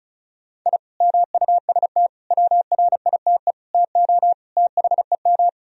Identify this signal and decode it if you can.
Morse code, heard as a keyed tone, about 36 wpm, 700 Hz. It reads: I MUST WRITE TO THEM
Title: Sentences from Top 1000 Words - Review Part 1 - 35wpm